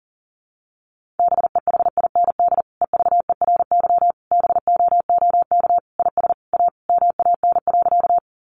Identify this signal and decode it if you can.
Morse code, heard as a keyed tone, about 38 wpm, 700 Hz. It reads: BEHIND EVERY BOOK IS A MAN.
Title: Sentences from Top 500 Words 40wpm